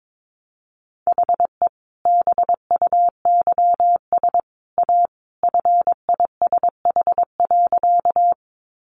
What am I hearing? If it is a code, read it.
Morse code, heard as a keyed tone, about 22 wpm, 700 Hz. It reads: HE BUYS A FISH.
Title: Additional Easy Phrases 22wpm